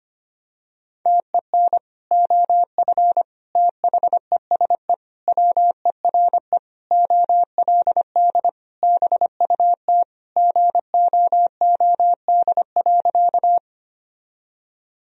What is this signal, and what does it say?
Morse code, heard as a keyed tone, about 25 wpm, 700 Hz. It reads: TEN OF THESE WERE OLD BUT GOOD.
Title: Sentences from Top 1000 Words - Review Part 2 - 25wpm